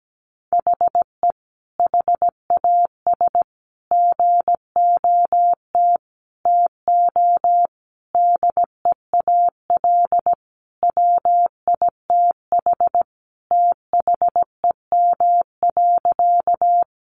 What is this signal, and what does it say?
Morse code, heard as a keyed tone, about 17 wpm, 700 Hz. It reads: HE HAS GOT TO DEAL WITH THEM.
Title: Sentences from Top 800 Words 17wpm